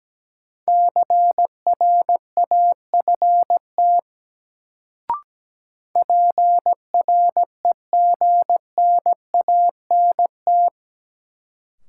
Morse code, heard as a keyed tone, about 17 wpm, 700 Hz. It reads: CRAFT PREGNANT
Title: Sentences from Top 1000 Words - Review Part 6 - New Word Drill - 17wpm